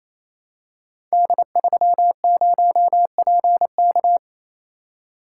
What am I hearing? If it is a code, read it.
Morse code, heard as a keyed tone, about 27 wpm, 700 Hz. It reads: D30PK
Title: Code Groups - 5 Letter:Numbers - 28wpm